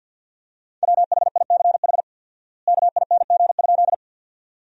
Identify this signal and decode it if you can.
Morse code, heard as a keyed tone, about 50 wpm, 700 Hz. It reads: WHICH KING?
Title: 4x Word-Spacing - Sentences from Top 400 Words - 50wpm